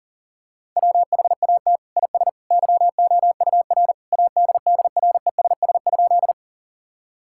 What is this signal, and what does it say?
Morse code, heard as a keyed tone, about 40 wpm, 700 Hz. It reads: WHAT IS YOUR ADDRESS?
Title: Sentences from Top 1000 Words - Review Part 6 - 40wpm